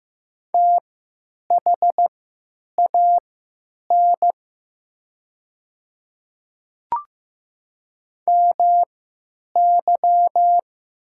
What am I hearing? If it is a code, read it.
Morse code, heard as a keyed tone, about 15 wpm, spaced as for 9 wpm, 700 Hz. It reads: THAN MY
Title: 3x Character Spacing - Top 100 Words - 15wpm